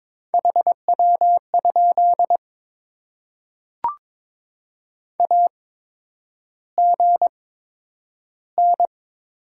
Morse code, heard as a keyed tone, about 22 wpm, spaced as for 9 wpm, 700 Hz. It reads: HW? AGN
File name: Top 20 QSO Elements - 8x Character Spacing - 22wpm